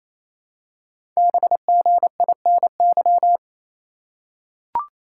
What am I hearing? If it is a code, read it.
Morse code, heard as a keyed tone, about 27 wpm, 700 Hz. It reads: BGINY
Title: Code Groups - 5 Letters - 28wpm